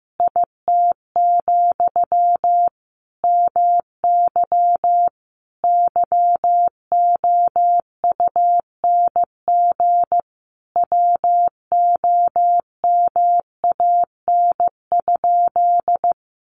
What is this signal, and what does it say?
Morse code, heard as a keyed tone, about 15 wpm, 700 Hz. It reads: IT, MY YOUNG WOMAN?
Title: Sentences from Top 1100 Words 15wpm